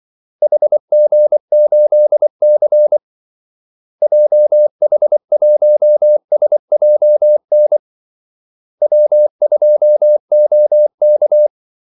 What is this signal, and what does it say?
Morse code, heard as a keyed tone, about 24 wpm, 600 Hz. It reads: HG8C JH1SJN W2OK
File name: Callsigns-24wpm